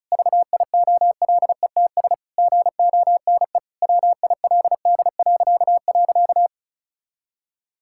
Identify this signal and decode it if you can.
Morse code, heard as a keyed tone, about 36 wpm, 700 Hz. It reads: VIOLETS GONE WILD..
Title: Pangram Challenge v2 - 35wpm